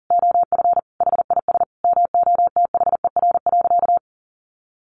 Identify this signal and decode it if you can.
Morse code, heard as a keyed tone, about 40 wpm, 700 Hz. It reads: OF HIS MOTHER.
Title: Sentences from Top 700 Words 40wpm